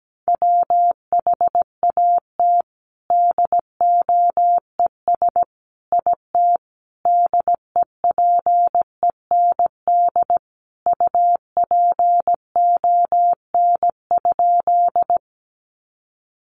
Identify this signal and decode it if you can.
Morse code, heard as a keyed tone, about 17 wpm, 700 Hz. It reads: WHAT DOES IT DEPEND UPON?
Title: Sentences from Top 1300 Words 17wpm